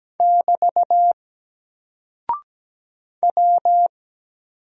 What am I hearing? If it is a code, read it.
Morse code, heard as a keyed tone, about 17 wpm, 700 Hz. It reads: <BT> W
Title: Single Character (Letter:Number:Punctuation:Prosign) Alt 17wpm